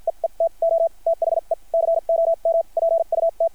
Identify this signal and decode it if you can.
Morse code, heard as a keyed tone, about 50 wpm, spaced as for 40 wpm, 660 Hz. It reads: EETO THE COMPUT
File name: archived / CRSnet